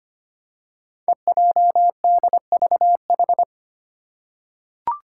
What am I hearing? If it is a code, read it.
Morse code, heard as a keyed tone, about 25 wpm, 700 Hz. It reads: EJDVH
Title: Five Challenging Letter-Number Combinations 25wpm